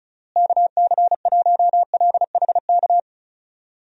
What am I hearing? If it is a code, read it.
Morse code, heard as a keyed tone, about 34 wpm, 700 Hz. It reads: KC1LHK
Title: USA Call Signs - 6 Characters - Repeated 3x - 35wpm